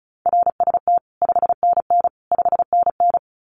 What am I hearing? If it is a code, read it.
Morse code, heard as a keyed tone, about 34 wpm, 700 Hz. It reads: RST 5NN 5NN
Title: Common QSO Phrases Encore 35wpm